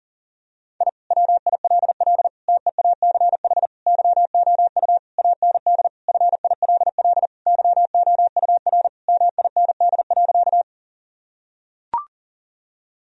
Morse code, heard as a keyed tone, about 40 wpm, 700 Hz. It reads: I WILL TEACH YOU AND FILL YOUR MIND.
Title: Sentences from Top 1000 Words - Review Part 3 - 40wpm